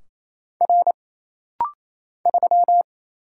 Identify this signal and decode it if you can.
Morse code, heard as a keyed tone, about 27 wpm, 700 Hz. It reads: R 3
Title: Single Letter-Number Alt - Rapid-Fire 28wpm